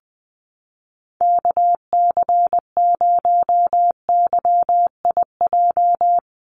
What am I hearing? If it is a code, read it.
Morse code, heard as a keyed tone, about 20 wpm, 700 Hz. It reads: KC0YIJ
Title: USA Call Signs 20wpm